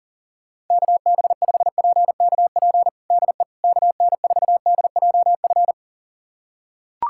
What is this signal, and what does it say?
Morse code, heard as a keyed tone, about 40 wpm, 700 Hz. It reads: KB5PKP DE KN4DJF
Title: USA Callsign DE Callsign - 6 Characters - 40wpm